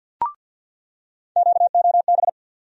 Morse code, heard as a keyed tone, about 46 wpm, 700 Hz. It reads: QOB
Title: USA Call Sign Postfixes - 3 Letters 50wpm